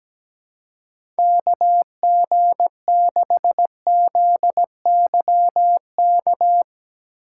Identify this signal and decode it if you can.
Morse code, heard as a keyed tone, about 17 wpm, 700 Hz. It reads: KG6ZYK
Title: USA Call Signs - 6 Characters - 6x Character Spacing - 17wpm